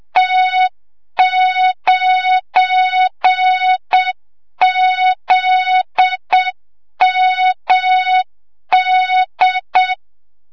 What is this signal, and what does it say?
Morse code, heard as a keyed tone, about 7 wpm, 3700 Hz, with noise. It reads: T9ZMD
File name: Morze-HA2MN / hang / 25 GYAKORLAT